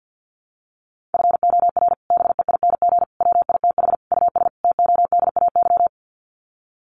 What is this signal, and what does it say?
Morse code, heard as a keyed tone, about 50 wpm, 700 Hz. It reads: FOR BEING WITH US TODAY
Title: Sets of 5 Words Encore2 50wpm